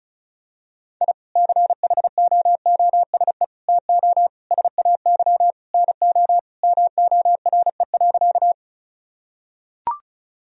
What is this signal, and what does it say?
Morse code, heard as a keyed tone, about 36 wpm, 700 Hz. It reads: I CHOOSE TO SAY NO MORE.
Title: Sentences from Top 1000 Words - Review Part 4 - 35wpm